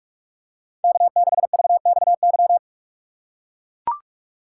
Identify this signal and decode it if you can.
Morse code, heard as a keyed tone, about 44 wpm, 700 Hz. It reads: K6VXY
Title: USA Call Signs - 5 Characters - 6x Character Spacing - 45wpm